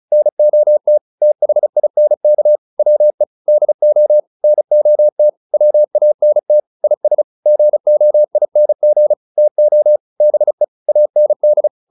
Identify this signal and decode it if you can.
Morse code, heard as a keyed tone, about 34 wpm, 600 Hz. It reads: NOT THINK WE DO NOT WANT IS GOING TO BE AND